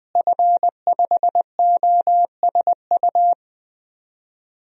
Morse code, heard as a keyed tone, about 20 wpm, 700 Hz. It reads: F5OSU
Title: DXCC-100 Call Signs - 5 Characters - Repeated 2x - 20wpm